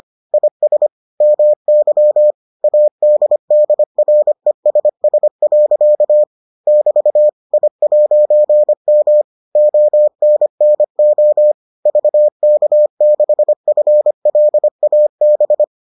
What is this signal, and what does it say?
Morse code, heard as a keyed tone, about 25 wpm, 600 Hz. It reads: IS MY ADDRESS. <BT> I'M ONNO VK6FLAB